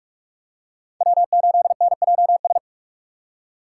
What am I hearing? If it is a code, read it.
Morse code, heard as a keyed tone, about 44 wpm, 700 Hz. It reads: W8NJS